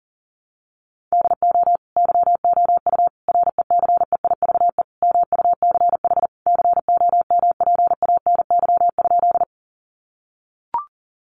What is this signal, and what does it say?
Morse code, heard as a keyed tone, about 37 wpm, 700 Hz. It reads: DO YOU RECEIVE MUCH COMPANY?
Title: Sentences from Top 1000 Words 40wpm